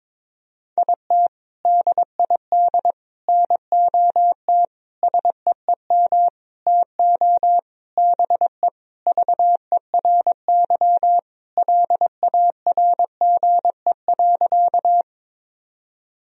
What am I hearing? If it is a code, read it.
Morse code, heard as a keyed tone, about 22 wpm, 700 Hz. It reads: IT DID NOT SEEM TO BE VERY LARGE.
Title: Sentences from Top 1000 Words - Review Part 1 - 22wpm